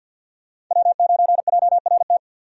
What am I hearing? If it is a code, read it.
Morse code, heard as a keyed tone, about 46 wpm, 700 Hz. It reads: W9JRT
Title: USA Call Signs - 5 Characters - 50wpm